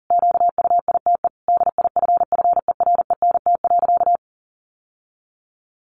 Morse code, heard as a keyed tone, about 40 wpm, 700 Hz. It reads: QUITE DIFFERENT.
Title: Sentences from Top 700 Words 40wpm